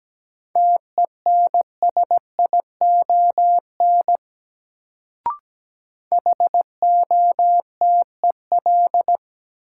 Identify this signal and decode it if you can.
Morse code, heard as a keyed tone, about 17 wpm, 700 Hz. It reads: TENSION HOTEL
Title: Sentences from Top 1600 Words - New Word Drill Alt - 17wpm